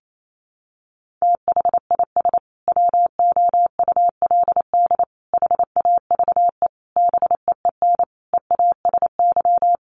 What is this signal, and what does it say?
Morse code, heard as a keyed tone, about 28 wpm, 700 Hz. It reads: THIS WOULD HAVE BEEN EASY
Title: Sentences from Top 800 Words 28wpm